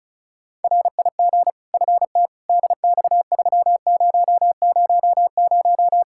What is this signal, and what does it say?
Morse code, heard as a keyed tone, about 36 wpm, 700 Hz. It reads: RIG FT DX3000
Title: Practice QSO - Exchange 2 - Info - 35wpm